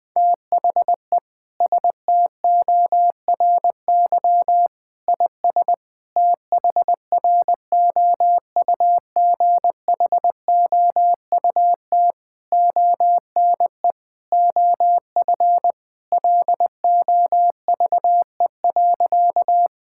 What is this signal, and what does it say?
Morse code, heard as a keyed tone, about 21 wpm, 700 Hz. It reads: THE STORY IS THROUGHOUT ONE OF LOVE.
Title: Sentences from Top 1000 Words - Review Part 5 - 20wpm